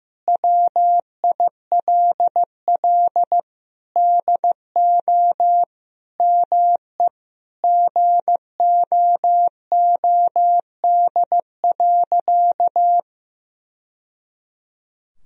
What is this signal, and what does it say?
Morse code, heard as a keyed tone, about 15 wpm, 700 Hz. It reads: WILL DO ME GOOD.
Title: Sentences from Top 1000 Words - Review Part 1 - 15wpm